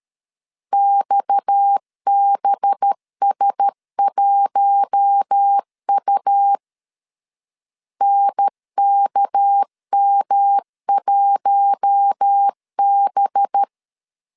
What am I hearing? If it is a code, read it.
Morse code, heard as a keyed tone, about 13 wpm, 800 Hz. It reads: XBS1U NKM1B
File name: CW / DET / 1 6 1500C S B 12wpm 3431